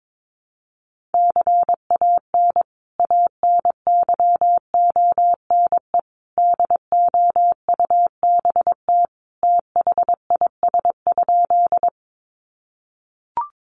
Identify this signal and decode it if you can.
Morse code, heard as a keyed tone, about 22 wpm, 700 Hz. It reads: CAN ANYONE DOUBT THIS?
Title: Sentences from Top 1500 Words 22wpm